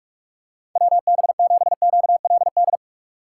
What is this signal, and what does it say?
Morse code, heard as a keyed tone, about 46 wpm, 700 Hz. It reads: WB7QLD
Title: USA Call Signs - 6 Characters - 4x Character Spacing - 45wpm